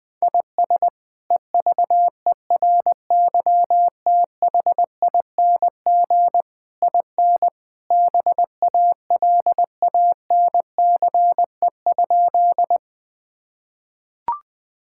Morse code, heard as a keyed tone, about 20 wpm, 700 Hz. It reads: IS EVERYTHING IN BALANCE?
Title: Sentences from Top 1000 Words - Review Part 6 - 20wpm